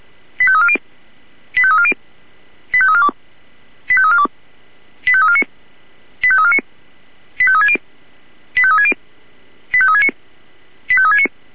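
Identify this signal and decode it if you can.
Five-tone selective call (ZVEI), heard as quick runs of five tones, tones of about 70 ms. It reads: 85340 85340 85342 85342 85349 85349 85360 85360 85369 85369
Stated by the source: Received by SDR as ZVEI five-tone selective call 85360.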